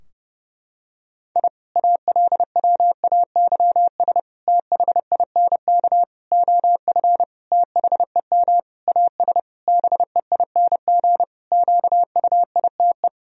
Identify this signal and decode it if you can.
Morse code, heard as a keyed tone, about 32 wpm, 700 Hz. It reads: I ALWAYS THINK OF THEM AS BEING QUITE
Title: Sentences from Top 1000 Words - Review Part 3 - 30wpm